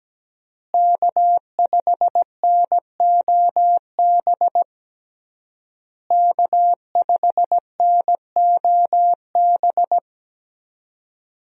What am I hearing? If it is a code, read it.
Morse code, heard as a keyed tone, about 17 wpm, 700 Hz. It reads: K5NOB K5NOB
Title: USA Call Signs - 5 Characters - Repeated 3x - 17wpm